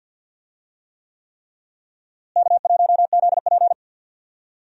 Morse code, heard as a keyed tone, about 52 wpm, 700 Hz. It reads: K1ZP
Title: USA Call Signs - 4 Characters - Repeated 3x - 50wpm